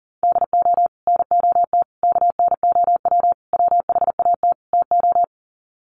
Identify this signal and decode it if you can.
Morse code, heard as a keyed tone, about 40 wpm, 700 Hz. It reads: DO NOT KNOW WHAT TO
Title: Sets of 5 Words 40wpm